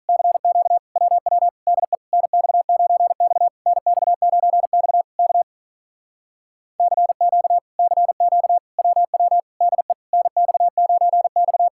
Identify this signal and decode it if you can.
Morse code, heard as a keyed tone, about 48 wpm, 700 Hz. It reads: CQ WW DE NX9X NX9X K CQ CQ WW DE NX9X